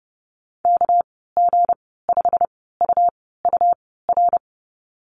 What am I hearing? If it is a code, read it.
Morse code, heard as a keyed tone, about 30 wpm, spaced as for 17 wpm, 700 Hz. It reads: KG5UUR
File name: Lesson 37g - Letter x - Call Signs 30wpm